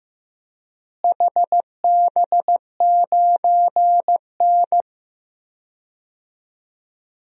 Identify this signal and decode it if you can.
Morse code, heard as a keyed tone, about 15 wpm, 700 Hz. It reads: HB9N
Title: DXCC-100 Call Signs - 4 Characters - Repeated 3x - 15wpm